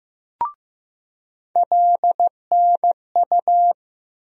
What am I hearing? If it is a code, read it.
Morse code, heard as a keyed tone, about 15 wpm, 700 Hz. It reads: LNU
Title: USA Call Sign Postfixes - 3 Letters 15wpm